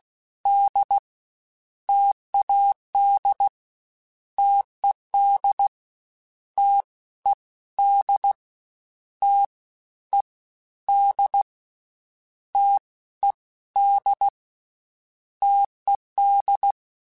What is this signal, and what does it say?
Morse code, heard as a keyed tone, about 16 wpm, 800 Hz. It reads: D TAD TED T E D T E D T E D TED
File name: Difficult Combos / Difficult Combos-mp3s / T AOEI @15